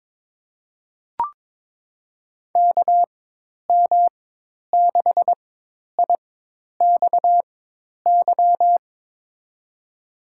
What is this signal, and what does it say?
Morse code, heard as a keyed tone, about 22 wpm, spaced as for 10 wpm, 700 Hz. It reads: KM6IXY
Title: USA Call Signs - 4x Character Spacing - 22wpm